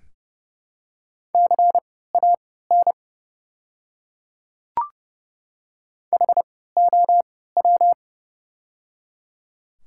Morse code, heard as a keyed tone, about 30 wpm, spaced as for 17 wpm, 700 Hz. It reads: CAN HOW
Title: 3x Character Spacing - Top 100 Words - 30wpm